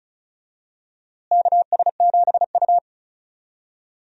Morse code, heard as a keyed tone, about 34 wpm, 700 Hz. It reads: KS7U
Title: USA Call Signs - 4 Characters - Repeated 3x - 35wpm